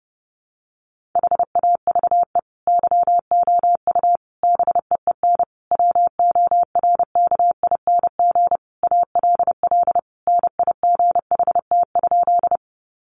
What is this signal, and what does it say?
Morse code, heard as a keyed tone, about 30 wpm, 700 Hz. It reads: HAVE YOU BEEN WORKING ALL NIGHT?
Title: Sentences from Top 700 Words 30wpm